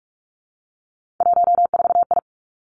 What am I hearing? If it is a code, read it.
Morse code, heard as a keyed tone, about 46 wpm, 700 Hz. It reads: 14I
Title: Three Challenging Letter-Number Combinations 45wpm